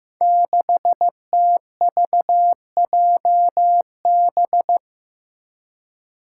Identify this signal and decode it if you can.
Morse code, heard as a keyed tone, about 15 wpm, 700 Hz. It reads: 6TVJB
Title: Five Challenging Letter-Number Combinations 15wpm